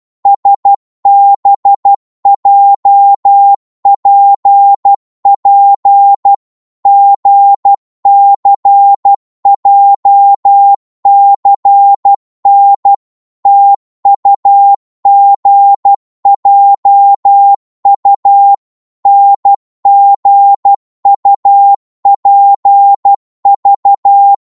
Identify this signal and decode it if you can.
Morse code, heard as a keyed tone, about 12 wpm, 800 Hz. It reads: SBJPP GCJCN TUGJU NGUPV
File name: Morse / ltr-12wpm